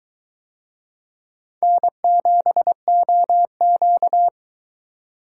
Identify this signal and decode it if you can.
Morse code, heard as a keyed tone, about 21 wpm, 700 Hz. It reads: N7OQ